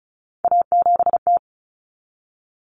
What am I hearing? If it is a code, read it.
Morse code, heard as a keyed tone, about 34 wpm, 700 Hz. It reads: A7T
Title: Common Cut Numbers - 3-Digit 35wpm